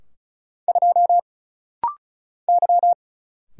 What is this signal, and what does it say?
Morse code, heard as a keyed tone, about 36 wpm, 700 Hz. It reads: 2 Y
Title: Single Letter-Number - Rapid-Fire 35wpm